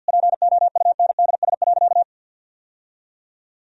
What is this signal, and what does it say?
Morse code, heard as a keyed tone, about 51 wpm, 700 Hz. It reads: POUNDS.
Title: Sentences from Top 1000 Words - Review Part 2 - 50wpm